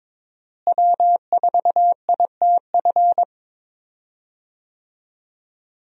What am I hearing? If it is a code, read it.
Morse code, heard as a keyed tone, about 22 wpm, 700 Hz. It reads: W4ITF
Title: USA Call Signs - 5 Characters - Repeated 3x - 22wpm